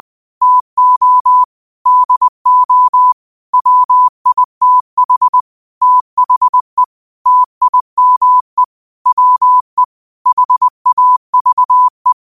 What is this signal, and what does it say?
Morse code, heard as a keyed tone, about 20 wpm, 1000 Hz. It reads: TO DO WITH THE TIME WE HAVE
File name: qotdfiles / 20wpm